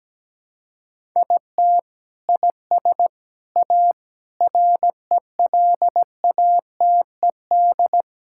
Morse code, heard as a keyed tone, about 17 wpm, 700 Hz. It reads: IT IS A RELATED
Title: Sentences from Top 1000 Words - Review Part 5 - 17wpm